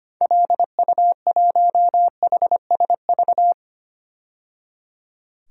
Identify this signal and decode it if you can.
Morse code, heard as a keyed tone, about 25 wpm, 700 Hz. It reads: LU1HSV
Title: DXCC-100 Call Signs - 6 Characters - 25wpm